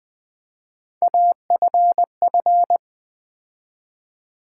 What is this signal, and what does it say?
Morse code, heard as a keyed tone, about 20 wpm, 700 Hz. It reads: AFF
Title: Build A Word - 3 Letters At A Time - 6-Letter Words - 20wpm